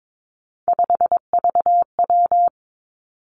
Morse code, heard as a keyed tone, about 22 wpm, 700 Hz. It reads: 5VW
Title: Three Challenging Letter-Number Combinations 22wpm